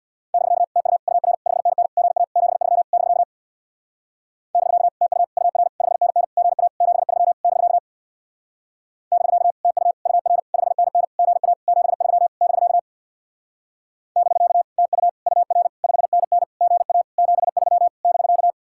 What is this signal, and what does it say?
Morse code, heard as a keyed tone, about 64 wpm, 700 Hz. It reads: <BK> TU UR 5NN GA 73 <BK> <BK> TU UR 5NN GA 73 <BK> <BK> TU UR 5NN GA 73 <BK> <BK> TU UR 5NN GA 73 <BK>